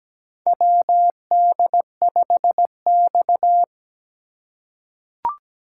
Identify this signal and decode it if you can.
Morse code, heard as a keyed tone, about 17 wpm, 700 Hz. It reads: WD5X
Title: USA Call Signs - 4 Characters - 4x Character Spacing - 17wpm